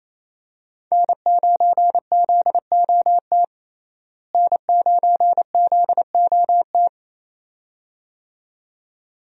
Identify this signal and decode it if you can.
Morse code, heard as a keyed tone, about 28 wpm, 700 Hz. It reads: N9ZOT N9ZOT
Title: USA Call Signs - 5 Characters - Repeated 2x - 28wpm